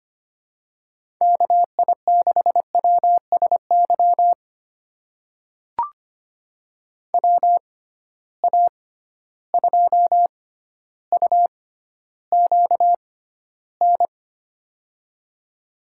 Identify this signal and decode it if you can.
Morse code, heard as a keyed tone, about 25 wpm, spaced as for 13 wpm, 700 Hz. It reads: KI6WSY WA2UQN
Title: USA Call Signs - 6 Characters - 6x Character Spacing - 25wpm